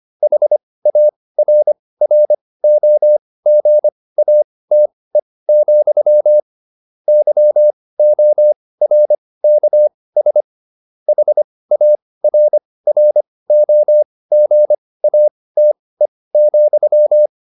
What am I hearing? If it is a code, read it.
Morse code, heard as a keyed tone, about 25 wpm, spaced as for 18 wpm, 600 Hz. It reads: HARROGATE, YORKS HARROGATE,